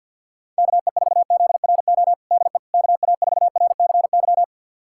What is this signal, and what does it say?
Morse code, heard as a keyed tone, about 48 wpm, 700 Hz. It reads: KE4ZRO DE KA4RCY
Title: USA Callsign DE Callsign 50wpm